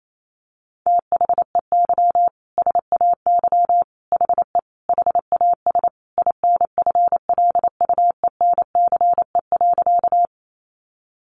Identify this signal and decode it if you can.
Morse code, heard as a keyed tone, about 28 wpm, 700 Hz. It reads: THEY SAY HE HAS INFLUENCE.
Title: Sentences from Top 1100 Words 28wpm